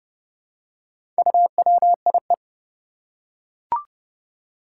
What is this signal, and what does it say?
Morse code, heard as a keyed tone, about 31 wpm, 700 Hz. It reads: UWIE
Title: Four Challenging Letter-Number Combinations 30wpm